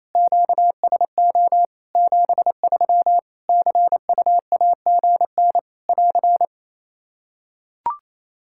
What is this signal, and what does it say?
Morse code, heard as a keyed tone, about 28 wpm, 700 Hz. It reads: QSO 73 CUAGN <AR>
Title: Practice QSO - Exchange 4 - Ending - 28wpm